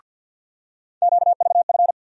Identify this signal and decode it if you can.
Morse code, heard as a keyed tone, about 49 wpm, 700 Hz. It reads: QVF
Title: USA Call Sign Postfixes - 3 Letters 50wpm